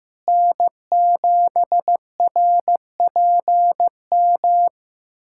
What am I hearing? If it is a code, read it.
Morse code, heard as a keyed tone, about 15 wpm, 700 Hz. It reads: N7RPM
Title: USA Callsign DE Callsign - 5 Characters - 15wpm